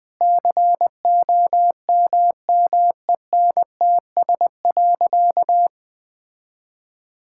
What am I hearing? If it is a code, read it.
Morse code, heard as a keyed tone, about 20 wpm, 700 Hz. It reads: COMMENTS.
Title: Sentences with 8-Word Phrases - Spoken Beginning - 20wpm